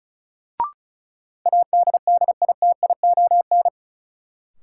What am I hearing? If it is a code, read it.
Morse code, heard as a keyed tone, about 36 wpm, 700 Hz. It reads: ADDITION
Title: Sets of 3 Words - New Word Drill - 35wpm